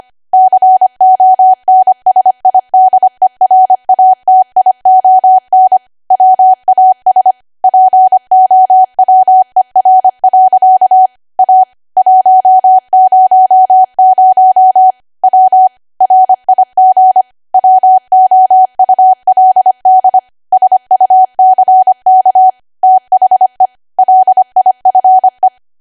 Morse code, heard as a keyed tone, about 25 wpm, 750 Hz. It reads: CONSIDERATION WAS POWER. A 100 W RIG WOULD SUCK THE LIFE
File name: Archive / 25 WPM / 25WPM